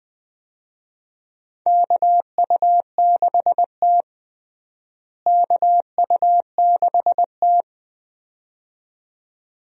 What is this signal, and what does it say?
Morse code, heard as a keyed tone, about 20 wpm, 700 Hz. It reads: KU6T KU6T
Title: USA Call Signs - 4 Characters - Repeated 2x - 20wpm